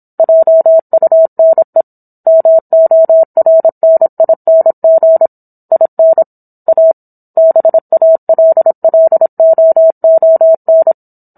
Morse code, heard as a keyed tone, about 27 wpm, 650 Hz. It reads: JUNE MORNING IN A BALLOON